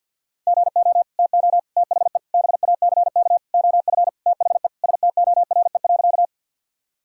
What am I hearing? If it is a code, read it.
Morse code, heard as a keyed tone, about 52 wpm, 700 Hz. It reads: GO TO THE BACK OF THE STORE.
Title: Sentences from Top 1000 Words - Review Part 5 - 50wpm